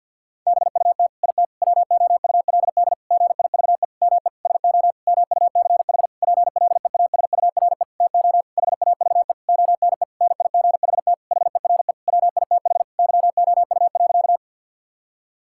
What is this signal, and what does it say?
Morse code, heard as a keyed tone, about 50 wpm, 700 Hz. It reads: BUT IT WOULD GIVE ME SO MUCH PLEASURE TO HAVE ONE NIGHT HERE WITH YOU.
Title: Sentences from Top 1600 Words 50wpm